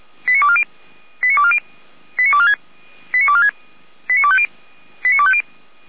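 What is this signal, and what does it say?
Five-tone selective call (ZVEI), heard as quick runs of five tones, tones of about 70 ms. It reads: 89240 89240 89246 89250 89250
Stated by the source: ZVEI five-tone selective call 89160 received by SDR, from the audio archive of feuerwehr-orlamuende.org fire brigade.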